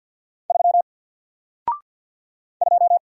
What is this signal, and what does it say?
Morse code, heard as a keyed tone, about 48 wpm, 700 Hz. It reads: 3 2
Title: Single Character (Letter:Number:Punctuation:Prosign) Alt 50wpm